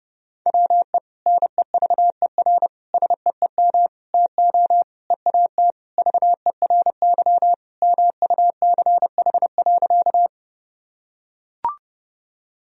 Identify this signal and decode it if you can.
Morse code, heard as a keyed tone, about 30 wpm, 700 Hz. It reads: WE NEVER SEEM TO EAT VERY MUCH.
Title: Sentences from Top 1000 Words - Review Part 1 - 30wpm